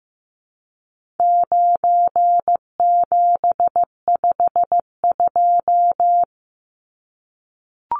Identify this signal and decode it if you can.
Morse code, heard as a keyed tone, about 15 wpm, 700 Hz. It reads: 9752
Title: Numbers - 4-Digit 15wpm